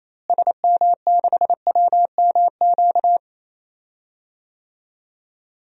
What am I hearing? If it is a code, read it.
Morse code, heard as a keyed tone, about 28 wpm, 700 Hz. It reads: SM6WMQ
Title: DXCC-100 Call Signs - 6 Characters - Repeated 3x - 28wpm